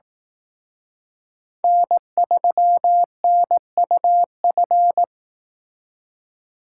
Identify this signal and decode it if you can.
Morse code, heard as a keyed tone, about 18 wpm, 700 Hz. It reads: N3NUF
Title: Speed Racing - USA 1x3 Call Signs 15wpm